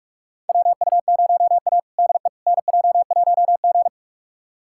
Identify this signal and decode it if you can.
Morse code, heard as a keyed tone, about 46 wpm, 700 Hz. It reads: WU0A DE NJ1G
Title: USA Callsign DE Callsign - 4 Characters - 45wpm